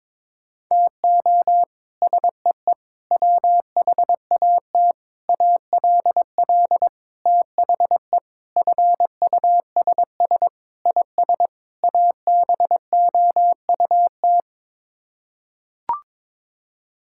Morse code, heard as a keyed tone, about 22 wpm, 700 Hz. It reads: TO SEE WHAT ALL THE FUSS IS ABOUT
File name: practice / 22wpm / Sets of 8 Words - 22wpm